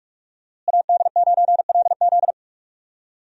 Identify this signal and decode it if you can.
Morse code, heard as a keyed tone, about 41 wpm, 700 Hz. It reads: AD9LZ
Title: USA Call Signs - 5 Characters - Repeated 3x - 45wpm